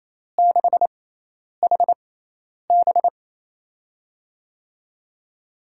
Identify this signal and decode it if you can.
Morse code, heard as a keyed tone, about 28 wpm, spaced as for 10 wpm, 700 Hz. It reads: 6HB
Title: USA Call Signs - 5 Characters - 6x Character Spacing - 28wpm